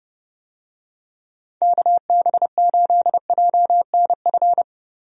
Morse code, heard as a keyed tone, about 30 wpm, 700 Hz. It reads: KB8JNF
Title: USA Call Signs - 6 Characters - 30wpm